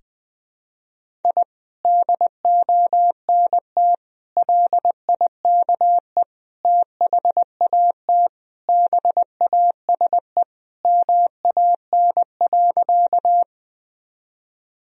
Morse code, heard as a keyed tone, about 21 wpm, 700 Hz. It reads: I DONT LIKE THAT BASE MAN.
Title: Sentences from Top 1000 Words - Review Part 1 - 20wpm